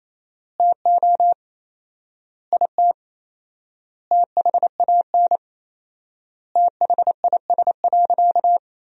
Morse code, heard as a keyed tone, about 28 wpm, 700 Hz. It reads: TO IT THAN THIS.
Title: 4x Word-Spacing - Sentences from Top 100 Words - 28wpm